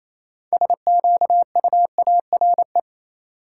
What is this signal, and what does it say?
Morse code, heard as a keyed tone, about 28 wpm, 700 Hz. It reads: SQUARE
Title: Six Letter Words - 28wpm